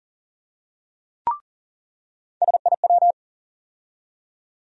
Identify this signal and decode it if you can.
Morse code, heard as a keyed tone, about 40 wpm, 700 Hz. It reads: SIW
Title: USA Call Sign Postfixes - 3 Letters 40wpm